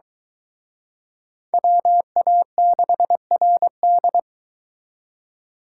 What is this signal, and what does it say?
Morse code, heard as a keyed tone, about 22 wpm, 700 Hz. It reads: WA6RD